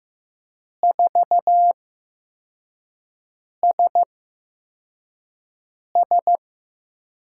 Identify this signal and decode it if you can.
Morse code, heard as a keyed tone, about 15 wpm, spaced as for 4 wpm, 700 Hz. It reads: 4SS
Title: USA Call Signs - 5 Characters - 8x Character Spacing - 15wpm